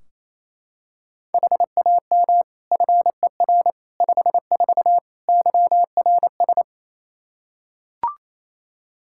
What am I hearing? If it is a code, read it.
Morse code, heard as a keyed tone, about 28 wpm, 700 Hz. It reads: HAM FER 54 YRS
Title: K1USN SST QSO Practice 28wpm